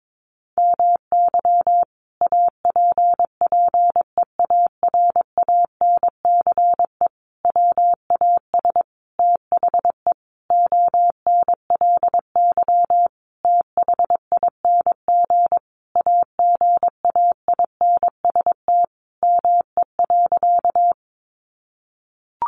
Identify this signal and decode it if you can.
Morse code, heard as a keyed tone, about 22 wpm, 700 Hz. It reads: MY APPEARANCE WAS THE ONLY THING AGAINST ME.
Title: Sentences from Top 1100 Words 22wpm